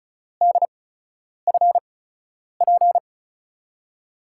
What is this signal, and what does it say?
Morse code, heard as a keyed tone, about 36 wpm, spaced as for 10 wpm, 700 Hz. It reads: DFP
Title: USA Call Signs - 6 Characters - 8x Character Spacing - 35wpm